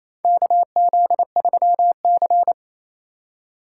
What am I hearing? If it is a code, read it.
Morse code, heard as a keyed tone, about 28 wpm, 700 Hz. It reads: KZ3C
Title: USA Call Signs - 3x Character Spacing - 28wpm